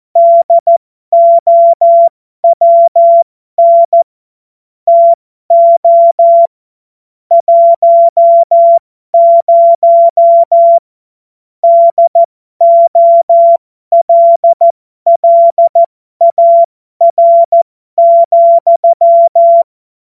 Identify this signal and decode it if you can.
Morse code, heard as a keyed tone, about 14 wpm, spaced as for 12 wpm, 680 Hz. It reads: DOWN TO 10 DOLLAR,